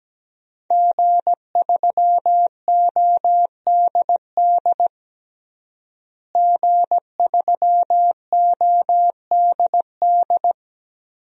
Morse code, heard as a keyed tone, about 17 wpm, 700 Hz. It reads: G3ODD G3ODD
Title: DXCC-100 Call Signs - 5 Characters - Repeated 3x - 17wpm